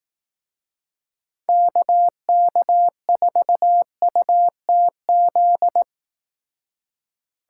Morse code, heard as a keyed tone, about 18 wpm, 700 Hz. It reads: KK4UTZ
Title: Speed Racing - USA 2x3 Call Signs 15wpm